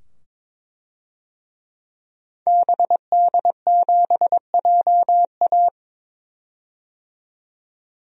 Morse code, heard as a keyed tone, about 22 wpm, 700 Hz. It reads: BD7JA
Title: DXCC-100 Call Signs - 5 Characters - 22wpm